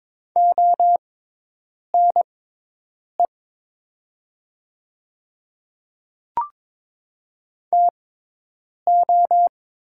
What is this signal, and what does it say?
Morse code, heard as a keyed tone, about 23 wpm, spaced as for 8 wpm, 700 Hz. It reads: ONE TO
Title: 6x Character Spacing - Top 100 Words - 22wpm